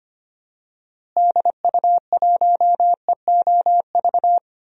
Morse code, heard as a keyed tone, about 25 wpm, 700 Hz. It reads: DU1EOV
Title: DXCC-100 Call Signs - 6 Characters - Repeated 2x - 25wpm